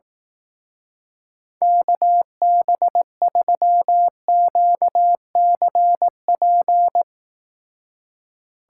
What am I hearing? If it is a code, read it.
Morse code, heard as a keyed tone, about 18 wpm, 700 Hz. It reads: KB3QCP
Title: Speed Racing - USA 2x3 Call Signs 15wpm